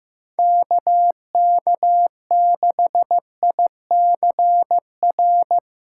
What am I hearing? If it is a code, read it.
Morse code, heard as a keyed tone, about 17 wpm, 700 Hz. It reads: KK6ICR
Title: Speed Racing - USA 2x3 Call Signs 15wpm